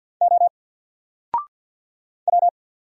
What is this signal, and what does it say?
Morse code, heard as a keyed tone, about 49 wpm, 700 Hz. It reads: O W
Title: Single Letter-Number Alt 50wpm